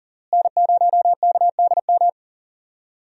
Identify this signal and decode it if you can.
Morse code, heard as a keyed tone, about 37 wpm, 700 Hz. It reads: N0KDM
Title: USA Call Signs - 5 Characters - Repeated 2x - 40wpm